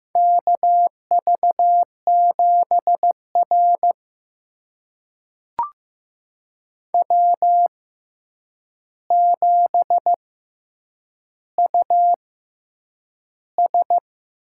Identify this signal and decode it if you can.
Morse code, heard as a keyed tone, about 15 wpm, spaced as for 8 wpm, 700 Hz. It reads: KV7R W7US
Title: USA Call Signs - 4 Characters - 6x Character Spacing - 15wpm